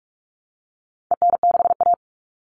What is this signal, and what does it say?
Morse code, heard as a keyed tone, about 44 wpm, 700 Hz. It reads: EN6A
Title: All Cut Numbers - 4-Digit 45wpm